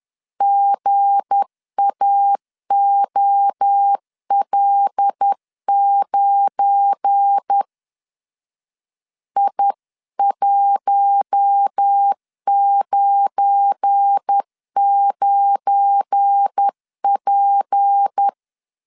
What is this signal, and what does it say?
Morse code, heard as a keyed tone, about 10 wpm, 800 Hz. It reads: GAOL9 I199P
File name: LEZIONI / 7 GF9Z / 7 GF9 10wpm 2747